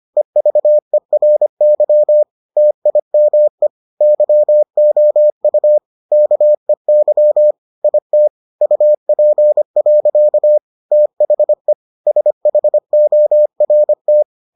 Morse code, heard as a keyed tone, about 25 wpm, 600 Hz. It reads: EVERY TIME YOU KEY IT UP. THE SHORT